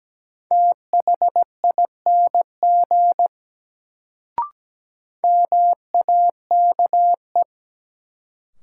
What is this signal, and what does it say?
Morse code, heard as a keyed tone, about 17 wpm, 700 Hz. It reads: THING MAKE
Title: Sentences from Top 100 Words - New Word Drill - 17wpm